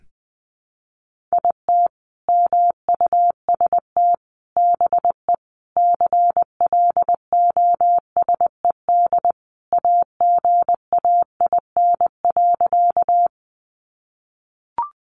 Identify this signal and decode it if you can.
Morse code, heard as a keyed tone, about 20 wpm, 700 Hz. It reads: IT MUST BE CLOSED AGAIN.
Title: Sentences from Top 1400 Words 20wpm